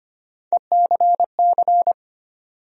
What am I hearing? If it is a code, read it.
Morse code, heard as a keyed tone, about 25 wpm, 700 Hz. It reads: ECC
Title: USA Call Sign Postfixes - 3 Letters 25wpm